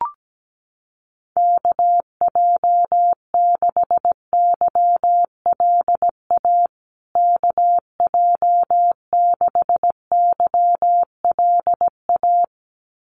Morse code, heard as a keyed tone, about 17 wpm, 700 Hz. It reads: KJ6YLA KJ6YLA
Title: USA Call Signs Repeated 2x 17wpm